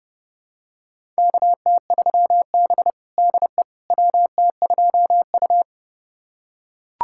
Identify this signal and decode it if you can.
Morse code, heard as a keyed tone, about 30 wpm, 700 Hz. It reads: KT3B DE WT2U
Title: USA Callsign DE Callsign - 4 Characters - 30wpm